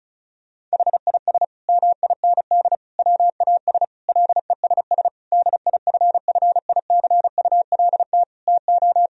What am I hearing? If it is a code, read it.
Morse code, heard as a keyed tone, about 34 wpm, 700 Hz. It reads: HIS MIND WAS LESS DIFFICULT TO
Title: Sentences from Top 1000 Words - Review Part 2 - 35wpm